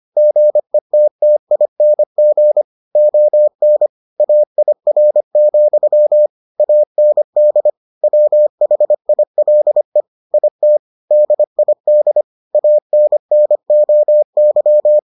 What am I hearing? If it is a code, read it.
Morse code, heard as a keyed tone, about 25 wpm, 600 Hz. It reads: GETTING ON AIR, AND WHILE IT DID ANNOY